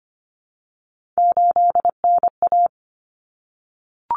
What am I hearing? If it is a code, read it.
Morse code, heard as a keyed tone, about 25 wpm, 700 Hz. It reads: 8NA
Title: Common Cut Numbers - 3-Digit 25wpm